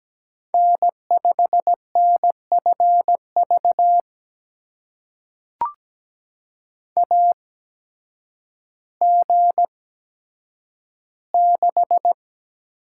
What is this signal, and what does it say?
Morse code, heard as a keyed tone, about 17 wpm, spaced as for 9 wpm, 700 Hz. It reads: N5NFV AG6
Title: USA Call Signs - 5 Characters - 8x Character Spacing - 17wpm